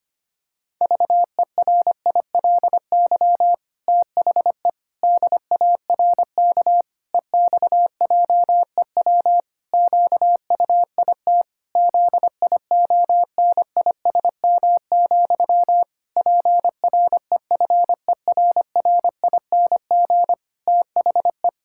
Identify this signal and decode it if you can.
Morse code, heard as a keyed tone, about 25 wpm, 700 Hz. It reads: VERILY THE DARK EXJEW QUIT ZIONISM, PREFERRING THE
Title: Pangram Challenge v2 - 25wpm